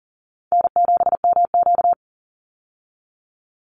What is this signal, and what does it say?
Morse code, heard as a keyed tone, about 40 wpm, 700 Hz. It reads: N7MQ
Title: Easy Call Signs Repeated 3x - 40wpm